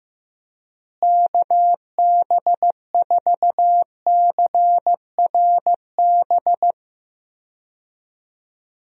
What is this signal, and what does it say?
Morse code, heard as a keyed tone, about 15 wpm, 700 Hz. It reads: KB4CRB
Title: USA Call Signs - 6 Characters - Repeated 2x - 15wpm